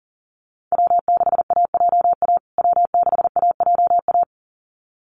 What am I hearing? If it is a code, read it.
Morse code, heard as a keyed tone, about 40 wpm, 700 Hz. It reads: W6AJA W6AJA
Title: USA Call Signs Repeated 2x 40wpm